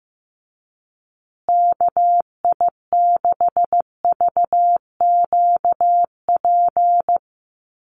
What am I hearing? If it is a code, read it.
Morse code, heard as a keyed tone, about 15 wpm, 700 Hz. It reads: KI6VQP
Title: USA Call Signs Repeated 3x 15wpm